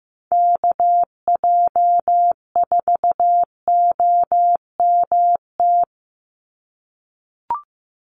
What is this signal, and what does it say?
Morse code, heard as a keyed tone, about 15 wpm, 700 Hz. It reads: KJ4OMT
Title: USA Call Signs 15wpm